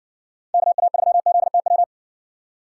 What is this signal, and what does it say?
Morse code, heard as a keyed tone, about 59 wpm, 700 Hz. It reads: KA3ZTW